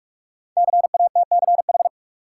Morse code, heard as a keyed tone, about 44 wpm, 700 Hz. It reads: CATCH
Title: Sentences from Top 1500 Words - New Word Drill - 45wpm